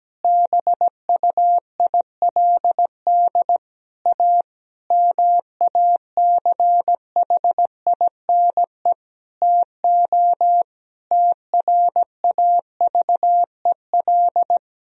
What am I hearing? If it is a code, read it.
Morse code, heard as a keyed tone, about 17 wpm, 700 Hz. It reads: BUILD A MACHINE TO TRAVEL
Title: Sentences from Top 1000 Words - Review Part 2 - 17wpm